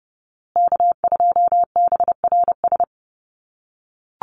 Morse code, heard as a keyed tone, about 30 wpm, 700 Hz. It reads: K2BRS
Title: USA Call Signs Repeated 2x 30wpm